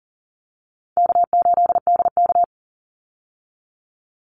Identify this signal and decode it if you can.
Morse code, heard as a keyed tone, about 40 wpm, 700 Hz. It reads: K8DK
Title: Call Signs - 40wpm